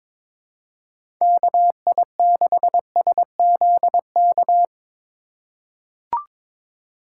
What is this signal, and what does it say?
Morse code, heard as a keyed tone, about 22 wpm, 700 Hz. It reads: KI6SZK
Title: USA Call Signs - 6 Characters - 4x Character Spacing - 22wpm